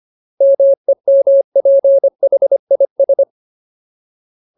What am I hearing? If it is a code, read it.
Morse code, heard as a keyed tone, about 25 wpm, 550 Hz. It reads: MEMPHIS
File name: Ultimate Word List / 25WPM Ultimate